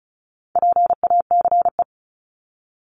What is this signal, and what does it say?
Morse code, heard as a keyed tone, about 34 wpm, 700 Hz. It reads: PACE
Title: Top 1751-2000 Words 35wpm